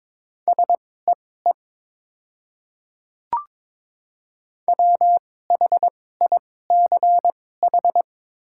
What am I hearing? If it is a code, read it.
Morse code, heard as a keyed tone, about 22 wpm, spaced as for 16 wpm, 700 Hz. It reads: SEE WHICH
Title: 2x Character Spacing - Top 100 Words - 22wpm